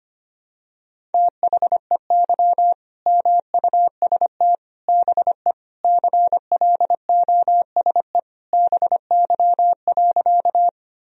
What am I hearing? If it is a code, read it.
Morse code, heard as a keyed tone, about 25 wpm, 700 Hz. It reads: THEY MUST BE CLOSE BY.
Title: Sentences from Top 1000 Words - Review Part 1 - 25wpm